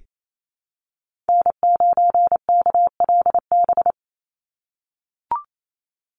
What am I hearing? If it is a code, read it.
Morse code, heard as a keyed tone, about 28 wpm, 700 Hz. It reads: N9KLB
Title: USA Call Signs 28wpm